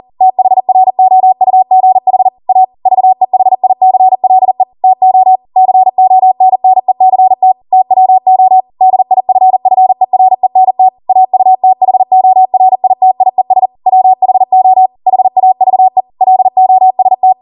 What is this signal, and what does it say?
Morse code, heard as a keyed tone, about 40 wpm, 750 Hz. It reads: THROUGH A VEHICLE TO CONNECT TWO DIFFERENT AUTHORITIES WHO HAVE LOST